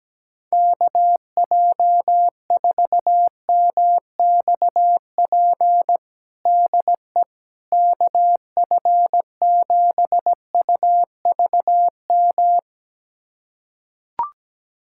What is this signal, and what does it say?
Morse code, heard as a keyed tone, about 17 wpm, 700 Hz. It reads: KJ4MXP DE KF7UVM
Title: USA Callsign DE Callsign 17wpm